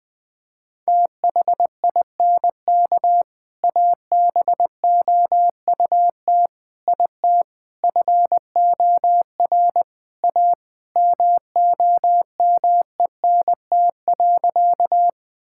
Morse code, heard as a keyed tone, about 20 wpm, 700 Hz. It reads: THINK ABOUT IT FOR A MOMENT.
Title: Sentences with 3-Word Phrases - 20wpm